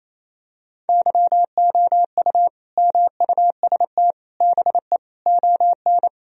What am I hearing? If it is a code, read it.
Morse code, heard as a keyed tone, about 28 wpm, 700 Hz. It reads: YOU MUST BE ON
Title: Sentences from Top 1000 Words - Review Part 4 - 28wpm